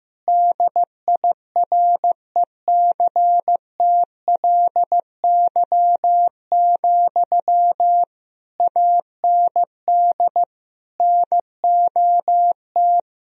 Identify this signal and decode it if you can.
Morse code, heard as a keyed tone, about 15 wpm, 700 Hz. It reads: DIRECTLY, AND NOT
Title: Sentences from Top 1000 Words - Review Part 1 - 15wpm